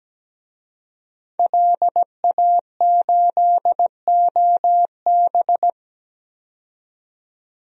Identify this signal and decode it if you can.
Morse code, heard as a keyed tone, about 17 wpm, 700 Hz. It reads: LA8OB
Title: DXCC-100 Call Signs - 5 Characters - 17wpm